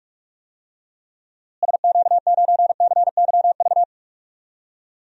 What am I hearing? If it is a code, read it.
Morse code, heard as a keyed tone, about 45 wpm, 700 Hz. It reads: SQ9CYV
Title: DXCC-100 Call Signs - 6 Characters - Repeated 3x - 45wpm